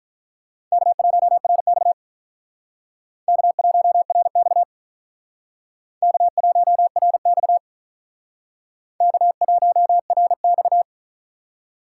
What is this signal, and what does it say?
Morse code, heard as a keyed tone, about 43 wpm, 700 Hz. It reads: K1RX K1RX K1RX K1RX
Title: Speed Racing - USA 1x2 Call Signs 35wpm